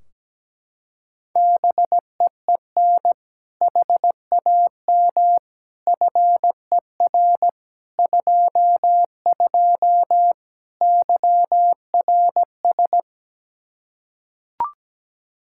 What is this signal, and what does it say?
Morse code, heard as a keyed tone, about 17 wpm, 700 Hz. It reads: BEEN HAM FER 22 YRS
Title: Practice QSO - Exchange 3 - More Info - 17wpm